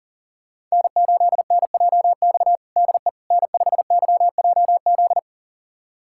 Morse code, heard as a keyed tone, about 40 wpm, 700 Hz. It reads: N8NJX DE N5YJZ
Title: USA Callsign DE Callsign - 5 Characters - 40wpm